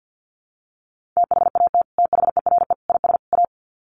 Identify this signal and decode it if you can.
Morse code, heard as a keyed tone, about 50 wpm, 700 Hz. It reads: THAT THERE IS A
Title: Sets of 4 Words 50wpm